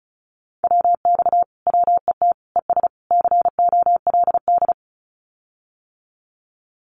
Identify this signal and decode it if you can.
Morse code, heard as a keyed tone, about 34 wpm, 700 Hz. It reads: WX WET ES COLD
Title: Common QSO Phrases Encore 35wpm